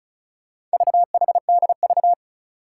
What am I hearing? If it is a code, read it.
Morse code, heard as a keyed tone, about 36 wpm, 700 Hz. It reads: VHDV
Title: Four Challenging Letter-Number Combinations 35wpm